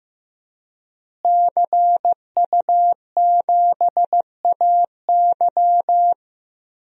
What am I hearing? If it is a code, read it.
Morse code, heard as a keyed tone, about 16 wpm, 700 Hz. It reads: CU7AY
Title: DXCC-100 Call Signs - 5 Characters - Repeated 3x - 15wpm